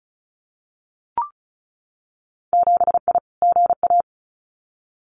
Morse code, heard as a keyed tone, about 36 wpm, 700 Hz. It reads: 7I GA
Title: Winter Field Day Exchanges Encore 35wpm